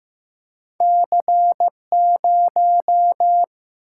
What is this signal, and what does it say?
Morse code, heard as a keyed tone, about 15 wpm, 700 Hz. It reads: C 0
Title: Single-Letter Single-Number 15wpm